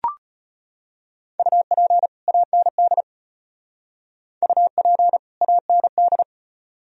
Morse code, heard as a keyed tone, about 31 wpm, 700 Hz. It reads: UP AND UP AND
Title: Speed-Racing - Sets of 2 Words 25wpm